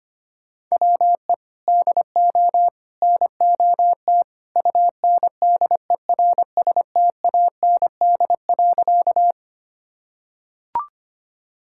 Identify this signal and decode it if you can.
Morse code, heard as a keyed tone, about 25 wpm, 700 Hz. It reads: WE DO NOT UNDERSTAND.
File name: practice / 25wpm / Sentences with 3-Word Phrases - 25wpm